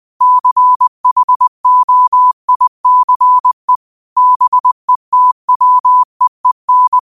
Morse code, heard as a keyed tone, about 20 wpm, 1000 Hz. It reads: CHOICE BETWEEN
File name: qotdfiles / 20wpm